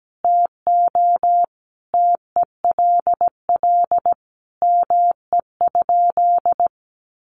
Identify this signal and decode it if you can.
Morse code, heard as a keyed tone, about 17 wpm, 700 Hz. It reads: TO TELL ME?
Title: Sentences from Top 600 Words 17wpm